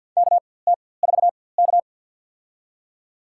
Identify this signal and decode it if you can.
Morse code, heard as a keyed tone, about 47 wpm, spaced as for 23 wpm, 700 Hz. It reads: KT4K
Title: USA Call Signs - 4x Character Spacing - 50wpm